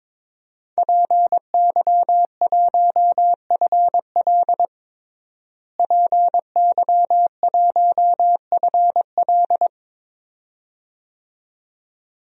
Morse code, heard as a keyed tone, about 22 wpm, 700 Hz. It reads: PY1FL PY1FL
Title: DXCC-100 Call Signs - 5 Characters - Repeated 2x - 22wpm